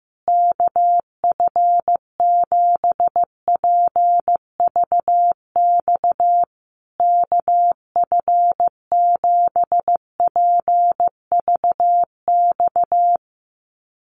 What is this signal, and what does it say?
Morse code, heard as a keyed tone, about 15 wpm, 700 Hz. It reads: KF7PVX KF7PVX
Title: USA Call Signs Repeated 3x 15wpm